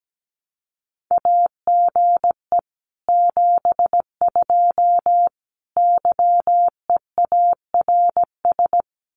Morse code, heard as a keyed tone, about 17 wpm, 700 Hz. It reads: AGE 72 YEARS
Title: Common QSO Phrases Encore 17wpm